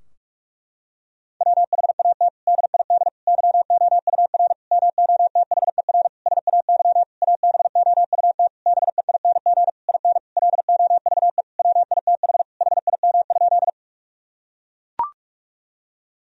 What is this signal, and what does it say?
Morse code, heard as a keyed tone, about 42 wpm, 700 Hz. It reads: WHAT DID YOUR MOTHER SAY ABOUT BEING IN LOVE WITH HIM?
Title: Sentences from Top 1000 Words - Review Part 3 - 45wpm